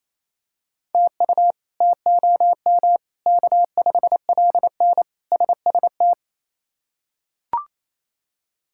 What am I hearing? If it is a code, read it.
Morse code, heard as a keyed tone, about 28 wpm, 700 Hz. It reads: TU TOM K5LN SST
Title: K1USN SST QSO Practice - 28wpm